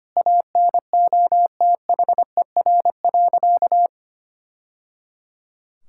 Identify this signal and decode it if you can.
Morse code, heard as a keyed tone, about 25 wpm, 700 Hz. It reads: ANOTHER.
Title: Sentences from Top 1000 Words - Review Part 6 - 25wpm